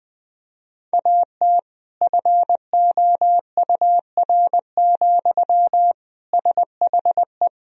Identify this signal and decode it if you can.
Morse code, heard as a keyed tone, about 20 wpm, 700 Hz. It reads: AT FOUR, SHE
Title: Sentences from Top 300 Words and Abbreviations 20wpm